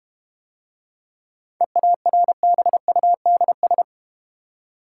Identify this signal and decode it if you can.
Morse code, heard as a keyed tone, about 33 wpm, 700 Hz. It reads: EARBUDS